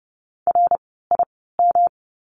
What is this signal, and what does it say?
Morse code, heard as a keyed tone, about 30 wpm, spaced as for 17 wpm, 700 Hz. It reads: RIM
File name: course / 30:10wpm / 108 Lesson 18f - Letter m - Words 30wpm